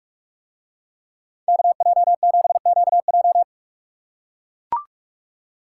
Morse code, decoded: KJ7QJ

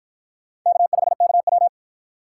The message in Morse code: K5KW

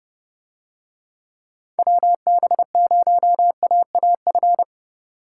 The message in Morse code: WB0AAF

